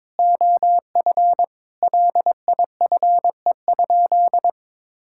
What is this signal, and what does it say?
Morse code, heard as a keyed tone, about 22 wpm, 700 Hz. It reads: OF LIFE?
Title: Sentences from Top 1000 Words - Review Part 5 - 22wpm